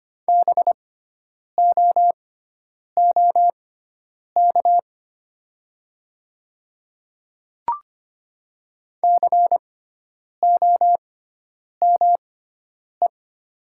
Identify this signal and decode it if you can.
Morse code, heard as a keyed tone, about 25 wpm, spaced as for 9 wpm, 700 Hz. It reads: BOOK COME
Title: 6x Character Spacing - Top 500 Words - 25wpm